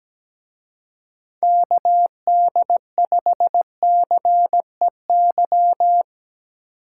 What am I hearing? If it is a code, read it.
Morse code, heard as a keyed tone, about 19 wpm, 700 Hz. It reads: KD5CEY
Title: Speed Racing - USA 2x3 Call Signs 17wpm